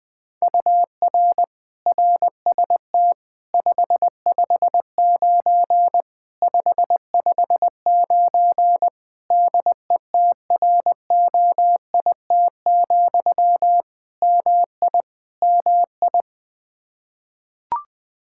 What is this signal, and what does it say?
Morse code, heard as a keyed tone, about 20 wpm, 700 Hz. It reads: UR RST 559 559 DETROIT, MI MI
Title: Practice QSO - Exchange 1 - Beginning - 20wpm